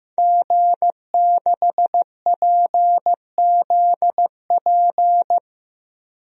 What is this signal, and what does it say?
Morse code, heard as a keyed tone, about 15 wpm, 700 Hz. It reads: G6PZP
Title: DXCC-100 Call Signs - 5 Characters - Repeated 2x - 15wpm